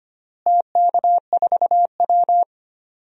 Morse code, heard as a keyed tone, about 25 wpm, 700 Hz. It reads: TK4W
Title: DXCC-100 Call Signs - 4 Characters - Repeated 3x - 25wpm